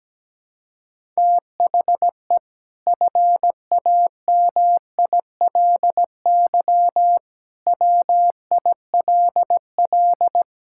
Morse code, heard as a keyed tone, about 17 wpm, 700 Hz. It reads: THE FAMILY WILL
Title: Sentences from Top 1000 Words - Review Part 5 - 17wpm